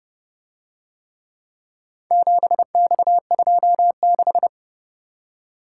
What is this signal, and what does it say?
Morse code, heard as a keyed tone, about 31 wpm, 700 Hz. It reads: 7X26